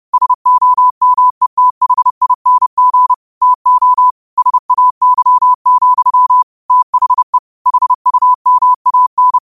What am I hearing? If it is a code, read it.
Morse code, heard as a keyed tone, about 30 wpm, 1000 Hz. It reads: SOMETHING TO SAY, THE HUMAN
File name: qotdfiles / 30wpm